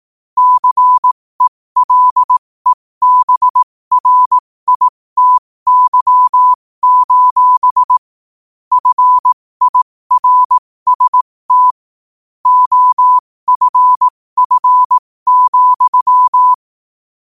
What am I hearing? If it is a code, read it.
Morse code, heard as a keyed tone, about 18 wpm, spaced as for 15 wpm, 1000 Hz. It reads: CELEBRITY: FIRST OFF,
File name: qotdfiles / 15wpm